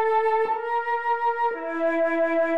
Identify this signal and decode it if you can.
There is no code.